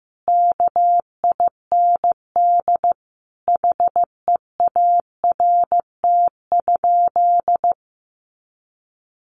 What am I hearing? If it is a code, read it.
Morse code, heard as a keyed tone, about 15 wpm, 700 Hz. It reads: KIND HEART?
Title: Sentences from Top 1200 Words 15wpm